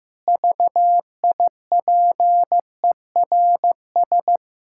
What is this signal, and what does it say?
Morse code, heard as a keyed tone, about 15 wpm, 700 Hz. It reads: VIPERS
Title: Pangram Challenge v2 - 15wpm